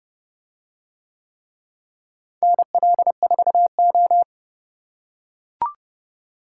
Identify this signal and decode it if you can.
Morse code, heard as a keyed tone, about 29 wpm, 700 Hz. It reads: NL4O